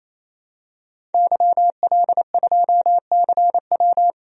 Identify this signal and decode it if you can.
Morse code, heard as a keyed tone, about 27 wpm, 700 Hz. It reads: YL2CW
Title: DXCC-100 Call Signs - 5 Characters - 28wpm